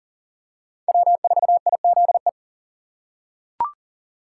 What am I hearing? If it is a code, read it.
Morse code, heard as a keyed tone, about 37 wpm, 700 Hz. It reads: W4IZE
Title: USA Call Signs - 5 Characters - 2x Character Spacing - 40wpm